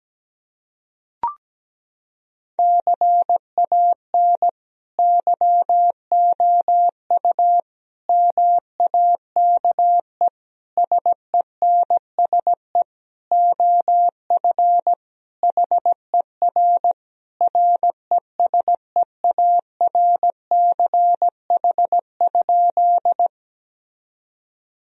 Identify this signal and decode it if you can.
Morse code, heard as a keyed tone, about 17 wpm, 700 Hz. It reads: CAN YOU MAKE SENSE OF HER RESEARCH?
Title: Sentences from Top 1000 Words - Review Part 4 - 17wpm